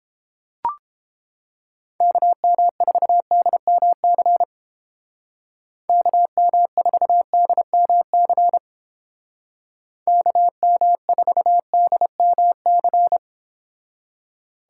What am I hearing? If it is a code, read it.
Morse code, heard as a keyed tone, about 30 wpm, 700 Hz. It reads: KM4DMC KM4DMC KM4DMC